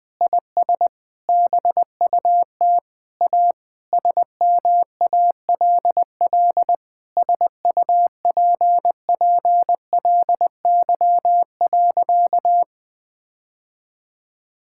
Morse code, heard as a keyed tone, about 20 wpm, 700 Hz. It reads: IS BUT A SMALL SUPPLY.
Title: Sentences from Top 1000 Words - Review Part 5 - 20wpm